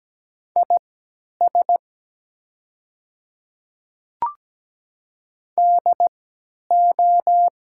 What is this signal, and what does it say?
Morse code, heard as a keyed tone, about 17 wpm, spaced as for 10 wpm, 700 Hz. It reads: IS DO
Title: 3x Character Spacing - Top 200 Words - 17wpm